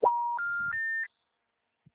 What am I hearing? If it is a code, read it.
Special information tone.
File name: out- -2007- - - .212